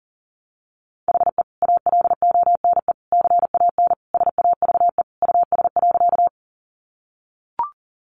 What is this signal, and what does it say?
Morse code, heard as a keyed tone, about 40 wpm, 700 Hz. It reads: HE ALONE CAN SAVE US.
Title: Sentences from Top 1100 Words 40wpm